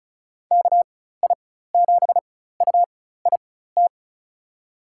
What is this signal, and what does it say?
Morse code, heard as a keyed tone, about 34 wpm, spaced as for 16 wpm, 700 Hz. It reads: KI7UIT
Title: USA Call Signs - 6 Characters - 4x Character Spacing - 35wpm